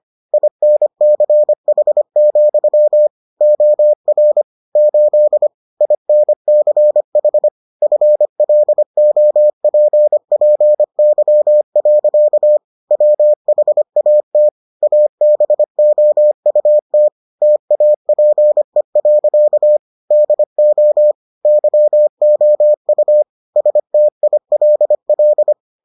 Morse code, heard as a keyed tone, about 25 wpm, 600 Hz. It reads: INCH, OR 8 INCH FLOPPY. WHAT ABOUT TAPE. DO YOU STILL